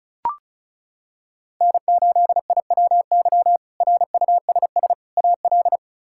Morse code, heard as a keyed tone, about 34 wpm, 700 Hz. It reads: N8IWY RUSS AL
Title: K1USN SST QSO Practice - 35wpm